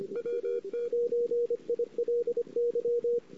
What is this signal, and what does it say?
Morse code, heard as a keyed tone, about 25 wpm, 480 Hz, with radio noise. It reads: W9ILY
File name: CW / 2008-ss-cw-k5tr-op-n5rz -22:00